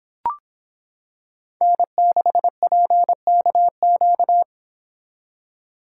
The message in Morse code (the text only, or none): N6PKQ